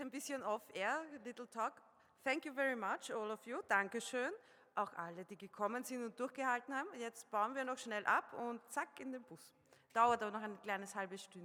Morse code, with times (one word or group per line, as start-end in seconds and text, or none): none